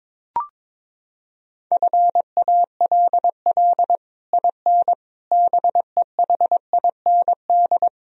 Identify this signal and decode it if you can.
Morse code, heard as a keyed tone, about 22 wpm, 700 Hz. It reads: FALL IN BEHIND
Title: Sentences from Top 1000 Words - Review Part 2 - 22wpm